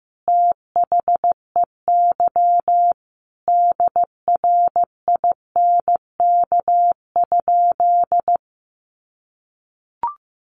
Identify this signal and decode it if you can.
Morse code, heard as a keyed tone, about 15 wpm, 700 Hz. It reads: THEY DRINK?
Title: Sentences from Top 1500 Words 15wpm